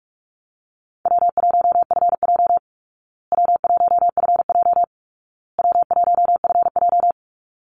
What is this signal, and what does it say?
Morse code, heard as a keyed tone, about 44 wpm, 700 Hz. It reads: W1FJ W1FJ W1FJ
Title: Easy Call Signs Repeated 3x - 45wpm